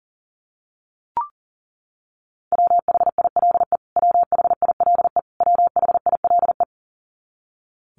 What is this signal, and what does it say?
Morse code, heard as a keyed tone, about 40 wpm, 700 Hz. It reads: WHILE WHILE WHILE
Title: Top 500 Words Repeated 40wpm